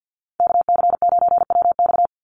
Morse code, heard as a keyed tone, about 48 wpm, 700 Hz. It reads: KC9WX
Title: USA Call Signs 50wpm